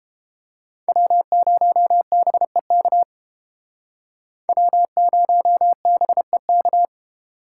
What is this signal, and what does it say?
Morse code, heard as a keyed tone, about 28 wpm, 700 Hz. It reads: W0BEK W0BEK